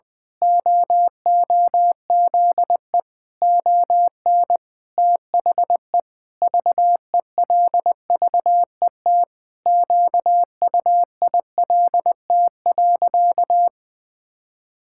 Morse code, heard as a keyed tone, about 20 wpm, 700 Hz. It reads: OOZE ON THE VELVET QUILT.
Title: Pangram Challenge v2 - 20wpm